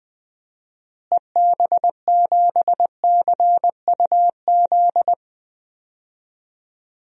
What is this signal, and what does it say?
Morse code, heard as a keyed tone, about 20 wpm, 700 Hz. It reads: EB7CUZ
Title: DXCC-100 Call Signs - 6 Characters - Repeated 2x - 20wpm